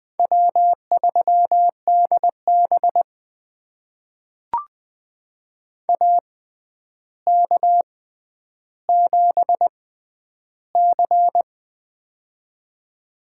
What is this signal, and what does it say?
Morse code, heard as a keyed tone, about 20 wpm, spaced as for 10 wpm, 700 Hz. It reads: W3DB AK7C
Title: USA Call Signs - 4 Characters - 6x Character Spacing - 20wpm